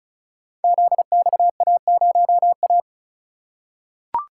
Morse code, heard as a keyed tone, about 36 wpm, 700 Hz. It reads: ZXA0A
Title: Code Groups - 5 Letter:Numbers - 35wpm